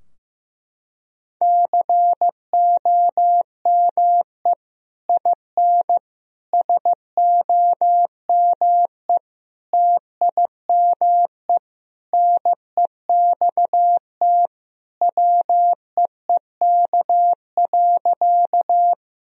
Morse code, heard as a keyed tone, about 15 wpm, 700 Hz. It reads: COME IN SOME TIME NEXT WEEK.
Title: Sentences from Top 1000 Words - Review Part 2 - 15wpm